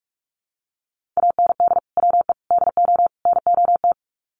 Morse code, heard as a keyed tone, about 44 wpm, 700 Hz. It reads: AND WE DO NOT
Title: Sets of 4 Words 45wpm